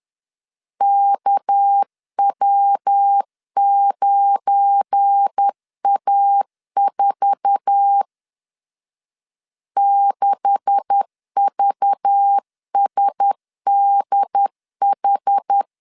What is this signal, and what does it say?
Morse code, heard as a keyed tone, about 10 wpm, 800 Hz. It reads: KW9A4 6VSDH
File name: CW / DET / 1 9 1500C S A 10wpm 4511